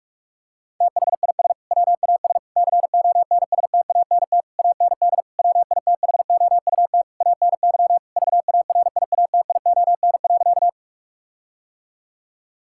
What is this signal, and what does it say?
Morse code, heard as a keyed tone, about 41 wpm, 700 Hz. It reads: THIS WAS CONSTANT AND WITHOUT ANY VARIATION.
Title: Sentences from Top 1600 Words 45wpm